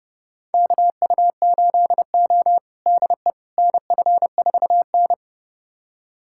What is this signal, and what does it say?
Morse code, heard as a keyed tone, about 30 wpm, 700 Hz. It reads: KU8O DE NF4N
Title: USA Callsign DE Callsign - 4 Characters - 30wpm